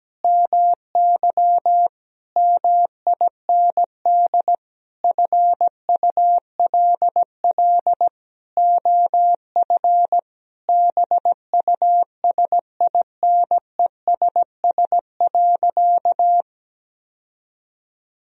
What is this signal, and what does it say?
Morse code, heard as a keyed tone, about 17 wpm, 700 Hz. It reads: MY MIND FULL OF BUSINESS.
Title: Sentences from Top 1000 Words - Review Part 3 - 17wpm